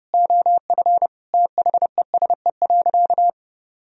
Morse code, heard as a keyed tone, about 32 wpm, 700 Hz. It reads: OF THESE.
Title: Sentences from Top 1000 Words - Review Part 1 - 30wpm